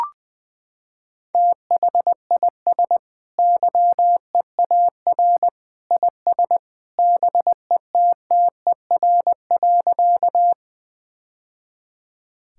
Morse code, heard as a keyed tone, about 20 wpm, 700 Hz. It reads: THIS YEAR IS BETTER.
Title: Sentences with 2-Word Phrases - 20wpm